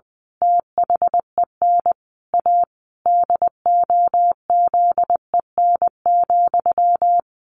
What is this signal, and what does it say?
Morse code, heard as a keyed tone, about 20 wpm, 700 Hz. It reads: THEN A DOZEN,
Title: A Sound Of Thunder 20wpm